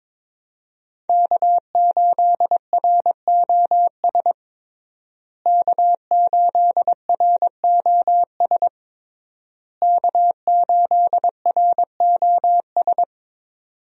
Morse code, heard as a keyed tone, about 23 wpm, 700 Hz. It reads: K8ROS K8ROS K8ROS
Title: USA Call Signs - 5 Characters - Repeated 3x - 22wpm